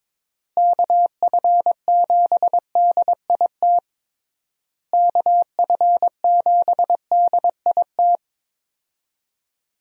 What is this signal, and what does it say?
Morse code, heard as a keyed tone, about 22 wpm, 700 Hz. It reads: KF7DIT KF7DIT
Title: USA Call Signs - 6 Characters - Repeated 3x - 22wpm